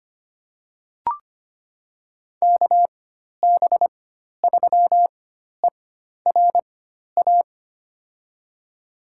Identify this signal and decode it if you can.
Morse code, heard as a keyed tone, about 25 wpm, spaced as for 12 wpm, 700 Hz. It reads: KB3ERA
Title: USA Call Signs - 6 Characters - 4x Character Spacing - 25wpm